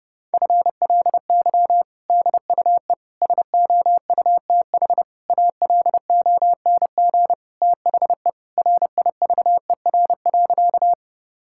Morse code, heard as a keyed tone, about 32 wpm, 700 Hz. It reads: FLY DUE SOUTH ALONG THE RIVER.
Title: Sentences from Top 1000 Words - Review Part 6 - 30wpm